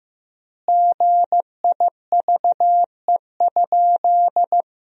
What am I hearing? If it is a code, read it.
Morse code, heard as a keyed tone, about 15 wpm, 700 Hz. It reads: GIVE?
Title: 3x Word-Spacing - Sentences from Top 200 Words - 15wpm